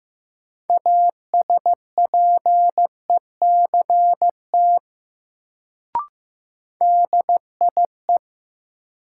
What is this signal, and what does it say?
Morse code, heard as a keyed tone, about 15 wpm, 700 Hz. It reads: ASPECT DIE
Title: Sentences from Top 1300 Words - New Word Drill Alt - 15wpm